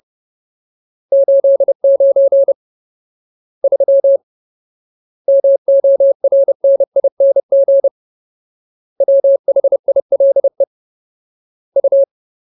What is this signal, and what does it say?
Morse code, heard as a keyed tone, about 30 wpm, 550 Hz. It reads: 89 3 MORNING WHILE U
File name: Ultimate Word List / 30WPM Ultimate